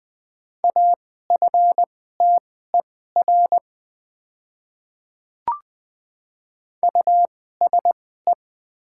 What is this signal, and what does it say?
Morse code, heard as a keyed tone, about 20 wpm, spaced as for 15 wpm, 700 Hz. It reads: AFTER USE
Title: 2x Character Spacing - Top 500 Words - 20wpm